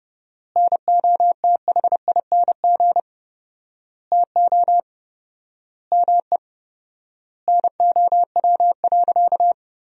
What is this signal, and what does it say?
Morse code, heard as a keyed tone, about 30 wpm, 700 Hz. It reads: NOTHING TO ME NOW.
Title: 4x Word-Spacing - Sentences from Top 500 Words - 30wpm